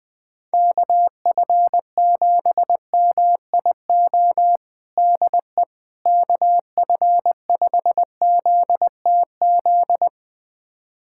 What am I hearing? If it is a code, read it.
Morse code, heard as a keyed tone, about 21 wpm, 700 Hz. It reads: KF7MIO DE KF5ZTZ
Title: USA Callsign DE Callsign 20wpm